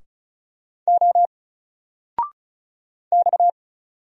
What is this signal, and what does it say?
Morse code, heard as a keyed tone, about 37 wpm, 700 Hz. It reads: O X
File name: Single-Letter-Number-35wpm